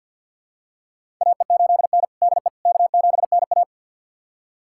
Morse code, heard as a keyed tone, about 53 wpm, 700 Hz. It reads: AE8N DE K7NA
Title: USA Callsign DE Callsign - 4 Characters - 50wpm